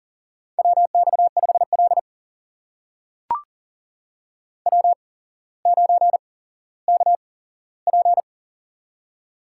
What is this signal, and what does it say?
Morse code, heard as a keyed tone, about 40 wpm, spaced as for 17 wpm, 700 Hz. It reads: WX5L W9KP
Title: USA Call Signs - 4 Characters - 8x Character Spacing - 40wpm